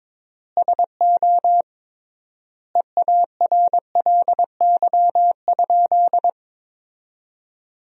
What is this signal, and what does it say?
Morse code, heard as a keyed tone, about 22 wpm, 700 Hz. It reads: SO EARLY?
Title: 3x Word-Spacing - Sentences from Top 400 Words - 22wpm